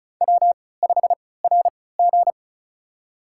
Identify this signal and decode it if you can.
Morse code, heard as a keyed tone, about 34 wpm, spaced as for 20 wpm, 700 Hz. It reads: W5RG
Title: USA Call Signs - 3x Character Spacing - 35wpm